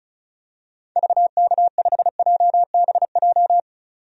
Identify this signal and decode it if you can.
Morse code, heard as a keyed tone, about 36 wpm, 700 Hz. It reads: VK5JBJ
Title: DXCC-100 Call Signs - 6 Characters - 35wpm